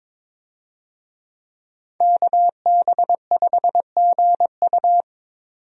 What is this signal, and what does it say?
Morse code, heard as a keyed tone, about 22 wpm, 700 Hz. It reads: KB5GU
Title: USA Call Signs - 5 Characters - 22wpm